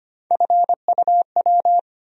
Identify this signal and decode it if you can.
Morse code, heard as a keyed tone, about 25 wpm, 700 Hz. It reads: FUW
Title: USA Call Sign Postfixes - 3 Letters 25wpm